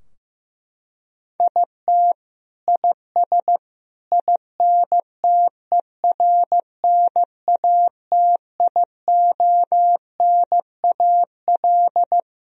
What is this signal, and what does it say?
Morse code, heard as a keyed tone, about 15 wpm, 700 Hz. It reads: IT IS INTERNATIONAL
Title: Sentences from Top 1000 Words - Review Part 4 - 15wpm